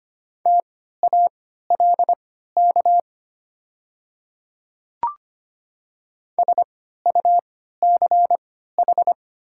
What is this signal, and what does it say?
Morse code, heard as a keyed tone, about 25 wpm, spaced as for 14 wpm, 700 Hz. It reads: TALK SUCH
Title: 3x Character Spacing - Top 500 Words - 25wpm